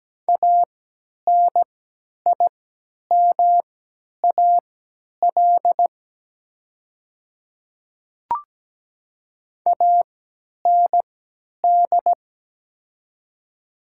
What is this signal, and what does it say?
Morse code, heard as a keyed tone, about 17 wpm, spaced as for 10 wpm, 700 Hz. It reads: ANIMAL AND
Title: 3x Character Spacing - Top 200 Words - 17wpm